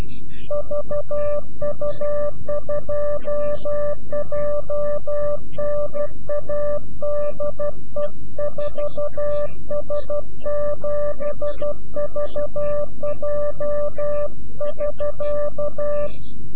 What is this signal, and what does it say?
Morse code, heard as a keyed tone, about 12 wpm, 590 Hz. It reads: VU2JNA DE 4S7VJ <SK>
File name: CW-QSO / CQ Call / CQ-QSO